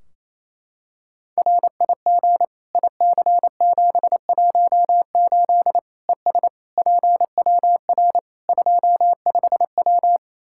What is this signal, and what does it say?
Morse code, heard as a keyed tone, about 28 wpm, 700 Hz. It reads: RIG IC718 ES PWR 25W
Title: Practice QSO - Exchange 2 - Info - 28wpm